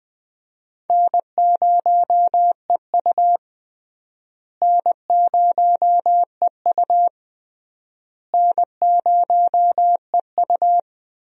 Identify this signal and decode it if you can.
Morse code, heard as a keyed tone, about 20 wpm, 700 Hz. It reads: N0EU N0EU N0EU
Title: USA Call Signs - 4 Characters - Repeated 3x - 20wpm